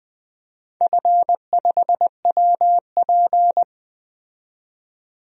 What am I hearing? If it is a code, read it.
Morse code, heard as a keyed tone, about 20 wpm, 700 Hz. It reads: F5WP
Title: DXCC-100 Call Signs - 4 Characters - 20wpm